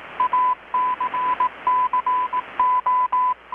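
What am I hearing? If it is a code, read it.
Morse code, heard as a keyed tone, about 18 wpm, 1000 Hz. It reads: ACCO
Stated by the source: Morse from KPH's Night of Nights XX maritime radiotelegraphy event, received in Chiapas, Mexico about 3,800 km away on a Kenwood R-600 with a grounded 10 m long-wire antenna.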